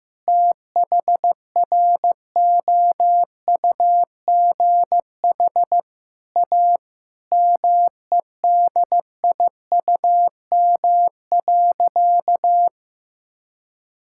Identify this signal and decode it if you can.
Morse code, heard as a keyed tone, about 15 wpm, 700 Hz. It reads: THROUGH A MEDIUM.
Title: Sentences from Top 1000 Words - Review Part 6 - 15wpm